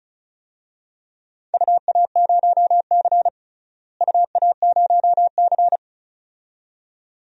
Morse code, heard as a keyed tone, about 36 wpm, 700 Hz. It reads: UA0C UA0C
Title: DXCC-100 Call Signs - 4 Characters - Repeated 2x - 35wpm